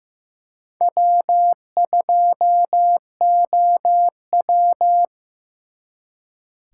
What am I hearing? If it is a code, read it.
Morse code, heard as a keyed tone, about 18 wpm, 700 Hz. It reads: W2OW
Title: Speed Racing - USA 1x2 Call Signs 15wpm